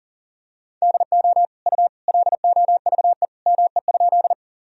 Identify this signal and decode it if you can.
Morse code, heard as a keyed tone, about 40 wpm, 700 Hz. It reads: DO U LOVE ME?
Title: Sentences from Top 400 Words and Abbreviations 40wpm